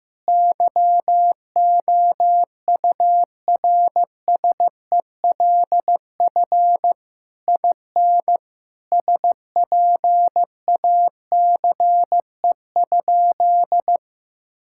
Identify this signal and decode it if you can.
Morse code, heard as a keyed tone, about 15 wpm, 700 Hz. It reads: YOURSELF IN SPACE?
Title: Sentences from Top 1000 Words - Review Part 5 - 15wpm